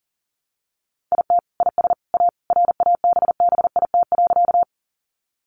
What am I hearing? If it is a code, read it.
Morse code, heard as a keyed tone, about 40 wpm, 700 Hz. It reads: IT IS A RABBIT.
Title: Easy Phrases 40wpm